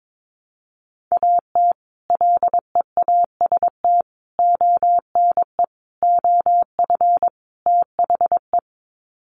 Morse code, heard as a keyed tone, about 22 wpm, 700 Hz. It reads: AT LEAST ONE OF THE
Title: Sets of 5 Words Encore 22wpm